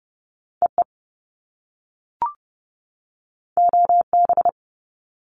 Morse code, heard as a keyed tone, about 30 wpm, 700 Hz. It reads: EE OB
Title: Top Two Letter Combinations in Words 30wpm